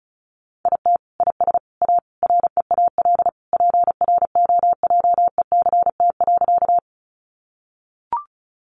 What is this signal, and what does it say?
Morse code, heard as a keyed tone, about 34 wpm, 700 Hz. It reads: IT IS A REAL PROJECT.
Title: Sentences from Top 1400 Words 35wpm